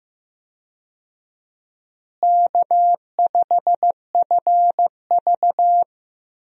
Morse code, heard as a keyed tone, about 16 wpm, 700 Hz. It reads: K5FV
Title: USA Call Signs - 4 Characters - Repeated 3x - 15wpm